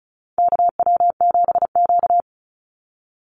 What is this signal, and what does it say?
Morse code, heard as a keyed tone, about 34 wpm, 700 Hz. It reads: KW7Q
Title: Call Signs Repeated 2x - 35wpm